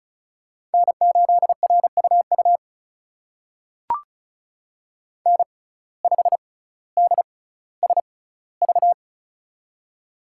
Morse code, heard as a keyed tone, about 36 wpm, spaced as for 18 wpm, 700 Hz. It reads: N8RUU N5DSV
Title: USA Call Signs - 6x Character Spacing - 35wpm